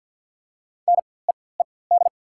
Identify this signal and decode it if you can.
Morse code, heard as a keyed tone, about 48 wpm, spaced as for 23 wpm, 700 Hz. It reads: NEED
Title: 4x Character Spacing - Top 500 Words - 50wpm